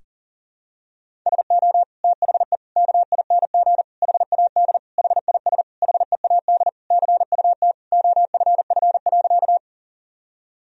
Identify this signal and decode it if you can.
Morse code, heard as a keyed tone, about 37 wpm, 700 Hz. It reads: SO THE KING HAD HIS HEAD CUT OFF.
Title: Sentences from Top 1000 Words - Review Part 1 - 40wpm